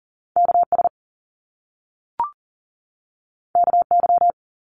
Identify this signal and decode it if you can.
Morse code, heard as a keyed tone, about 42 wpm, 700 Hz. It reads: KS KY
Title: US States 40wpm